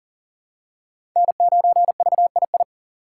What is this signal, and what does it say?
Morse code, heard as a keyed tone, about 40 wpm, 700 Hz. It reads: N9VII
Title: USA Call Signs - 5 Characters - 3x Character Spacing - 40wpm